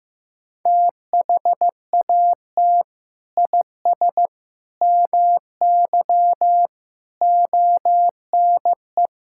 Morse code, heard as a keyed tone, about 15 wpm, 700 Hz. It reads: THAT IS MY ONE
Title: Sentences from Top 1000 Words - Review Part 5 - 15wpm